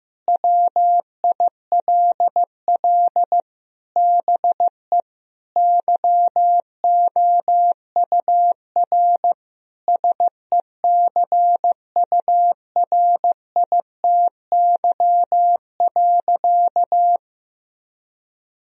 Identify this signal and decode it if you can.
Morse code, heard as a keyed tone, about 15 wpm, 700 Hz. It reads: WILL BE YOUR SECURITY.
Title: Sentences from Top 1000 Words - Review Part 4 - 15wpm